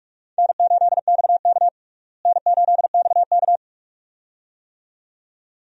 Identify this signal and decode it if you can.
Morse code, heard as a keyed tone, about 43 wpm, 700 Hz. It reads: N8XK N8XK
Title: USA Call Signs - 4 Characters - Repeated 2x - 45wpm